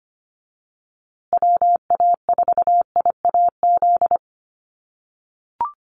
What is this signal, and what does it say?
Morse code, heard as a keyed tone, about 25 wpm, 700 Hz. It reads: WA4IAZ
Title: USA Call Signs Repeated 3x 25wpm